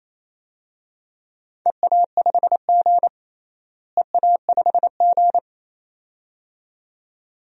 Morse code, heard as a keyed tone, about 28 wpm, 700 Hz. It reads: EA5G EA5G
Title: DXCC-100 Call Signs - 4 Characters - Repeated 2x - 28wpm